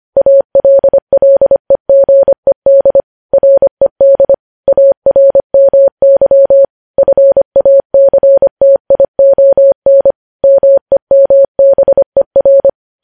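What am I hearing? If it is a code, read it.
Morse code, heard as a keyed tone, about 25 wpm, 550 Hz. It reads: ALLEGED RED ARMY FACTION MEMBER
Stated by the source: News headlines sent in Morse code at 25 wpm.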